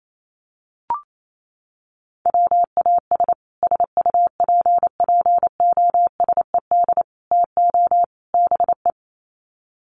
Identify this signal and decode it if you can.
Morse code, heard as a keyed tone, about 28 wpm, 700 Hz. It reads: WAS SUPPOSED TO BE
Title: Sets of 4 Words 28wpm